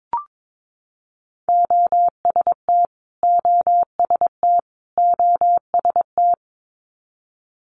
Morse code, heard as a keyed tone, about 22 wpm, 700 Hz. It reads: OST OST OST
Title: Multi-letter Phonemes Repeated 3x 22wpm